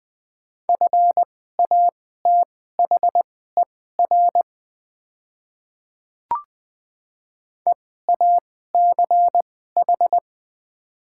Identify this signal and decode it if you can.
Morse code, heard as a keyed tone, about 20 wpm, spaced as for 15 wpm, 700 Hz. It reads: FATHER EACH
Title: 2x Character Spacing - Top 200 Words - 20wpm